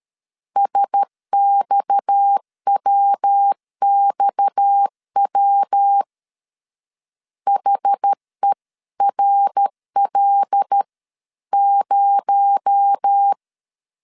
Morse code, heard as a keyed tone, about 13 wpm, 800 Hz. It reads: SXWXW HERL0
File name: CW / DET / 1 6 1500C S A 12wpm 3440